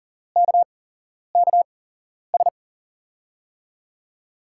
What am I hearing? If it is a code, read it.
Morse code, heard as a keyed tone, about 42 wpm, spaced as for 11 wpm, 700 Hz. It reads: KKS
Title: USA Call Signs - 6 Characters - 8x Character Spacing - 40wpm